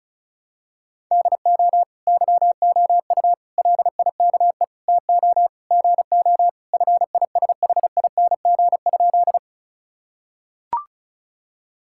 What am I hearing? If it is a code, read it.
Morse code, heard as a keyed tone, about 34 wpm, 700 Hz. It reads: DO YOU LIKE TO GO FISHING?
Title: Sentences from Top 1000 Words - Review Part 5 - 35wpm